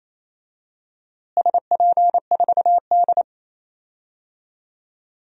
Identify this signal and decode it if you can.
Morse code, heard as a keyed tone, about 28 wpm, 700 Hz. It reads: SP4D
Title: DXCC-100 Call Signs - 4 Characters - Repeated 3x - 28wpm